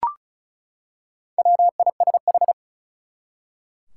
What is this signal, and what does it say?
Morse code, heard as a keyed tone, about 36 wpm, 700 Hz. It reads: WISH
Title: Four Letter Words - 35wpm